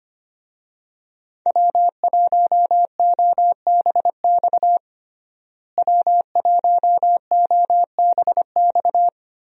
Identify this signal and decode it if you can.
Morse code, heard as a keyed tone, about 25 wpm, 700 Hz. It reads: W1OBX W1OBX
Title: USA Call Signs - 5 Characters - Repeated 3x - 25wpm